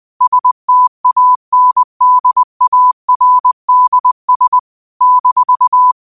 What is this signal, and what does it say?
Morse code, heard as a keyed tone, about 20 wpm, 1000 Hz. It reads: STANDARDS -
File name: qotdfiles / 20wpm